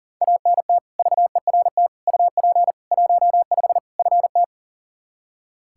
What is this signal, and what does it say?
Morse code, heard as a keyed tone, about 37 wpm, 700 Hz. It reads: ANT VERT UP 15 FT
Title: Practice QSO - Exchange 2 - Info - 40wpm